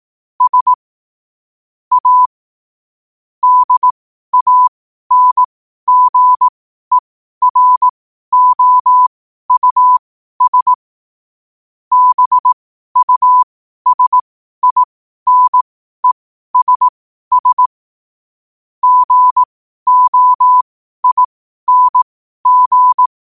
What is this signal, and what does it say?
Morse code, heard as a keyed tone, about 18 wpm, spaced as for 12 wpm, 1000 Hz. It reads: S A DANGEROUS BUSINESS GOING
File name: 12wpm